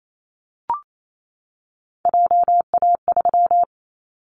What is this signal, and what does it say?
Morse code, heard as a keyed tone, about 28 wpm, 700 Hz. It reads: JA3
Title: Callsign Prefixes for Top 25 DXCC - 28wpm